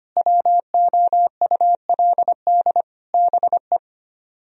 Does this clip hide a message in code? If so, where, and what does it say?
Yes: Morse code from 0.2 to 3.8 s, heard as a keyed tone, about 33 wpm, 700 Hz. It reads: WOULD BE